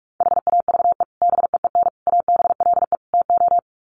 Morse code, heard as a keyed tone, about 45 wpm, 700 Hz. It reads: HAVE BEEN ABLE TO
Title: Sets of 4 Words 45wpm